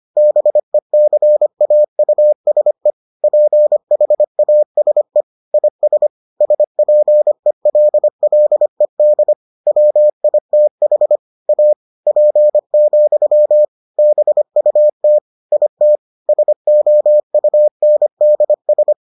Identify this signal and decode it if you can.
Morse code, heard as a keyed tone, about 25 wpm, 600 Hz. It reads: BECAUSE PHASE IS SPELLED WITH A P, BUT IT SOUNDS